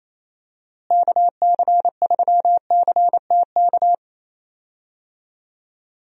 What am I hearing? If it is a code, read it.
Morse code, heard as a keyed tone, about 28 wpm, 700 Hz. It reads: KC3CTK
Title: USA Call Signs - 6 Characters - Repeated 3x - 28wpm